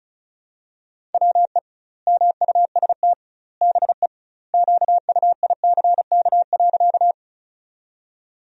Morse code, heard as a keyed tone, about 34 wpm, 700 Hz. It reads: WE MUST BE QUICK.
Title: 2x Word-Spacing - Sentences from Top 500 Words - 35wpm